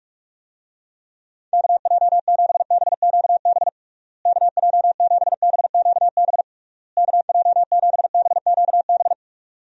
Morse code, heard as a keyed tone, about 46 wpm, 700 Hz. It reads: KJ7BQB KJ7BQB KJ7BQB
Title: USA Call Signs - 6 Characters - Repeated 3x - 45wpm